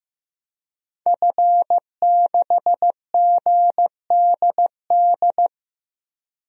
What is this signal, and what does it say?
Morse code, heard as a keyed tone, about 16 wpm, 700 Hz. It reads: F6GDD
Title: DXCC-100 Call Signs - 5 Characters - Repeated 3x - 15wpm